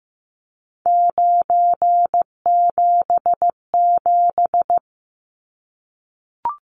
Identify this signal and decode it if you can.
Morse code, heard as a keyed tone, about 15 wpm, 700 Hz. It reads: 977
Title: Numbers - 3-Digit 15wpm